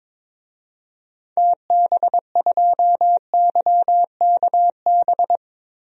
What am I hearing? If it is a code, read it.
Morse code, heard as a keyed tone, about 22 wpm, 700 Hz. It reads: TB2YKB
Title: DXCC-100 Call Signs - 6 Characters - Repeated 2x - 22wpm